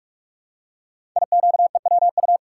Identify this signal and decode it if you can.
Morse code, heard as a keyed tone, about 44 wpm, 700 Hz. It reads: IQEWU
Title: Code Groups - 5 Letters - 45wpm